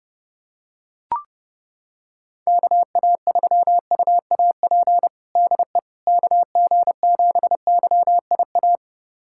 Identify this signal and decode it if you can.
Morse code, heard as a keyed tone, about 28 wpm, 700 Hz. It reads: KA3UAP DE KG7YIA
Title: USA Callsign DE Callsign - 6 Characters - 30wpm